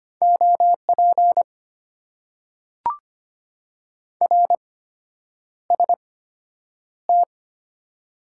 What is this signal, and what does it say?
Morse code, heard as a keyed tone, about 25 wpm, spaced as for 9 wpm, 700 Hz. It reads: OP RST
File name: Top 90 QSO Elements - 8x Character Spacing - 25wpm